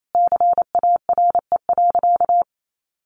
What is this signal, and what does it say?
Morse code, heard as a keyed tone, about 28 wpm, 700 Hz. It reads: CARE.
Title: Sentences from Top 600 Words 28wpm